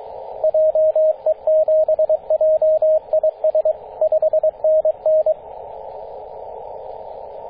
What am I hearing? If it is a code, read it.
Morse code, heard as a keyed tone, about 22 wpm, 630 Hz, with radio noise. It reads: JE7JIS 5NN
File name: 9m0o-24-891mhz-cw